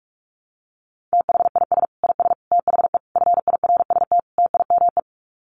Morse code, heard as a keyed tone, about 44 wpm, 700 Hz. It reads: THIS IS THE FIRST TIME
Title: Sets of 5 Words 45wpm